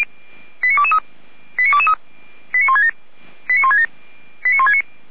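Five-tone selective call (ZVEI), heard as quick runs of five tones, tones of about 70 ms. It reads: 89223 89223 89167 89167 89169
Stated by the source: Received on SDR, ZVEI five-tone code 89121.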